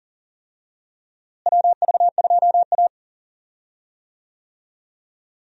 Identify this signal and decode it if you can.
Morse code, heard as a keyed tone, about 40 wpm, 700 Hz. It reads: WV2A
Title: USA Call Signs - 4 Characters - Repeated 2x - 40wpm